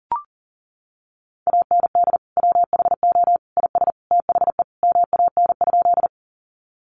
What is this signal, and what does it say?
Morse code, heard as a keyed tone, about 40 wpm, 700 Hz. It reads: AND WHO IS THE MAN?
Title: Sentences from Top 200 Words 40wpm